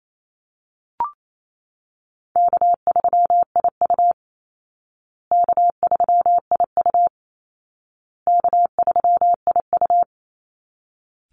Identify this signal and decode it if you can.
Morse code, heard as a keyed tone, about 28 wpm, 700 Hz. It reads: K3IU K3IU K3IU
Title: Easy Call Signs Repeated 3x - 28wpm